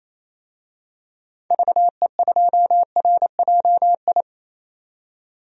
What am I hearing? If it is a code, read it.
Morse code, heard as a keyed tone, about 27 wpm, 700 Hz. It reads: VE2RJI
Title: DXCC-100 Call Signs - 6 Characters - 28wpm